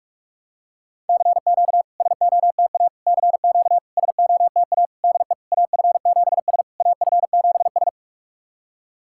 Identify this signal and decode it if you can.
Morse code, heard as a keyed tone, about 44 wpm, 690 Hz. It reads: CQ SOTA CQ SOTA DE AF7S AF7S
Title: SOTA QSO Practice 45wpm